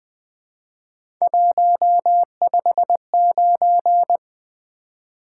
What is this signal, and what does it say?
Morse code, heard as a keyed tone, about 20 wpm, 700 Hz. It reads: 159
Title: Three Letter-Number Combinations - 20wpm